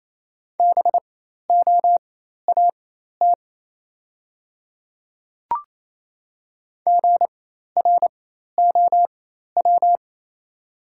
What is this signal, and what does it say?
Morse code, heard as a keyed tone, about 28 wpm, spaced as for 13 wpm, 700 Hz. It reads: BOAT GROW
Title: 4x Character Spacing - Top 500 Words - 28wpm